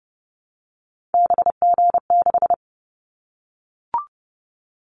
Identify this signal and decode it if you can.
Morse code, heard as a keyed tone, about 30 wpm, 700 Hz. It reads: BG6
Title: Callsign Prefixes for Top 100 DXCC - 30wpm